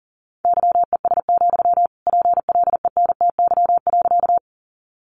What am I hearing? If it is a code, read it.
Morse code, heard as a keyed tone, about 40 wpm, 700 Hz. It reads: YES, PLENTY.
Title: Sentences from Top 1300 Words 40wpm